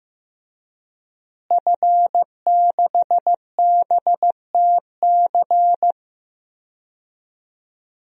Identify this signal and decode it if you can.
Morse code, heard as a keyed tone, about 15 wpm, 700 Hz. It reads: F6BTC
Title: DXCC-100 Call Signs - 5 Characters - Repeated 2x - 15wpm